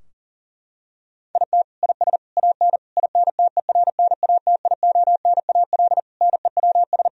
Morse code, heard as a keyed tone, about 37 wpm, 700 Hz. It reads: IT IS AN INTERNATIONAL NEWS
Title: Sentences from Top 1000 Words - Review Part 4 - 40wpm